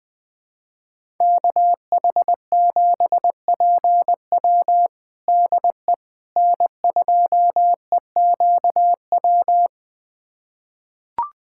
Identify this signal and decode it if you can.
Morse code, heard as a keyed tone, about 20 wpm, 700 Hz. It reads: KH7PW DE N2EQW
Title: USA Callsign DE Callsign - 5 Characters - 20wpm